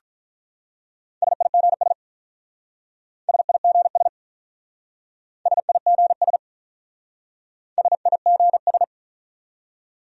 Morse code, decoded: SIGS SIGS SIGS SIGS

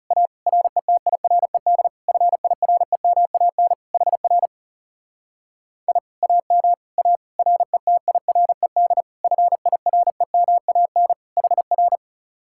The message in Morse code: A RETIRED FIREMAN HR I AM A RETIRED FIREMAN HR

